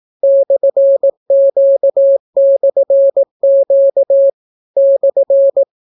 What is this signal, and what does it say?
Morse code, heard as a keyed tone, about 18 wpm, 550 Hz. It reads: /Q/Q /